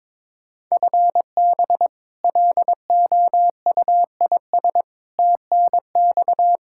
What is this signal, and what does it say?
Morse code, heard as a keyed tone, about 22 wpm, 700 Hz. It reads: FB LOUIS TNX
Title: K1USN SST QSO Practice 22wpm